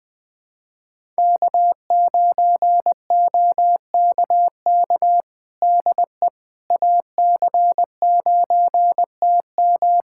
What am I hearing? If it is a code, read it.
Morse code, heard as a keyed tone, about 20 wpm, 700 Hz. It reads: K9OKK DE AC9TM
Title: USA Callsign DE Callsign - 5 Characters - 20wpm